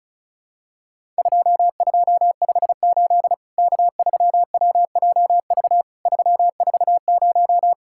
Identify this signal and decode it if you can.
Morse code, heard as a keyed tone, about 36 wpm, 700 Hz. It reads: 2258 K3WJV 340
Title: Worked All Europe DX Contest - QTC Exchange - 35wpm